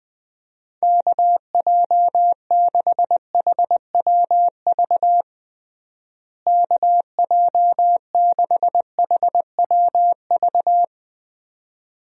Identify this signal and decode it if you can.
Morse code, heard as a keyed tone, about 20 wpm, 700 Hz. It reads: KJ6HWV KJ6HWV
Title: USA Call Signs - 6 Characters - Repeated 2x - 20wpm